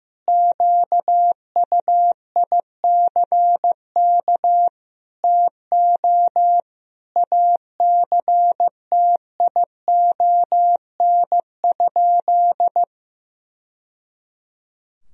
Morse code, heard as a keyed tone, about 15 wpm, 700 Hz. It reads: QUICK TO ACTION?
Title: Sentences from Top 1000 Words - Review Part 4 - 15wpm